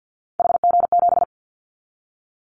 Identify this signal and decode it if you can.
Morse code, heard as a keyed tone, about 50 wpm, 700 Hz. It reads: HG7
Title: Callsign Prefixes for Top 50 DXCC - 50wpm